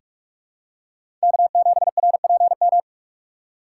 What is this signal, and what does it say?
Morse code, heard as a keyed tone, about 47 wpm, 700 Hz. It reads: K7RPM
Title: USA Call Signs - 5 Characters - 3x Character Spacing - 45wpm